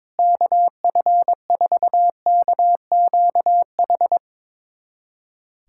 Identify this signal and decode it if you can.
Morse code, heard as a keyed tone, about 22 wpm, 700 Hz. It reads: KF4KQH
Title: USA Callsign DE Callsign - 6 Characters - 22wpm